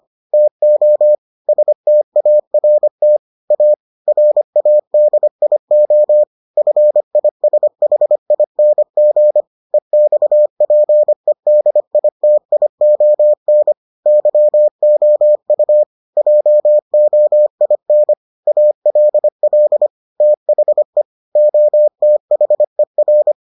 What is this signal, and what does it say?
Morse code, heard as a keyed tone, about 25 wpm, 600 Hz. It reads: TO START A RADIO FISHING EXPEDITION YOU JOIN ALL THE OTHER